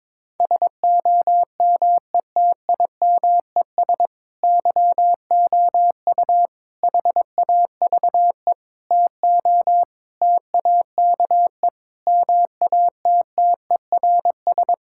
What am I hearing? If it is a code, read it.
Morse code, heard as a keyed tone, about 22 wpm, 700 Hz. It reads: SOMETIMES YOU HAVE TO TAKE MATTERS
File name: Sentences with 7-Word Phrases - 22wpm